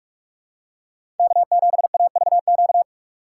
Morse code, heard as a keyed tone, about 47 wpm, 700 Hz. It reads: K7AVQ
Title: USA Call Signs - 5 Characters - 3x Character Spacing - 45wpm